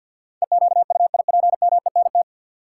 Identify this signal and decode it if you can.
Morse code, heard as a keyed tone, about 51 wpm, 700 Hz. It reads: EQUIPMENT
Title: Sentences from Top 700 Words - New Word Drill - 50wpm